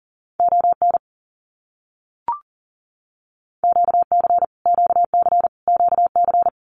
Morse code, heard as a keyed tone, about 40 wpm, 700 Hz. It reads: ON QC QC QC
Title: Canadian Provinces and Territories - Repeated 3x - 40wpm